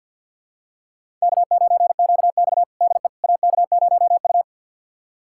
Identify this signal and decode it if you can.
Morse code, heard as a keyed tone, about 46 wpm, 700 Hz. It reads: K9QX DE AK0U
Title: USA Callsign DE Callsign - 4 Characters - 50wpm